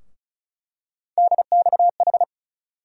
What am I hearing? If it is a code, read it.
Morse code, heard as a keyed tone, about 34 wpm, 700 Hz. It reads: DXH
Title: USA Call Sign Postfixes - 3 Letters 35wpm